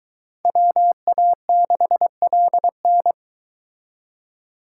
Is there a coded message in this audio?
Yes: Morse code from 0.5 to 3.1 s, heard as a keyed tone, about 22 wpm, 700 Hz. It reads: WA6LN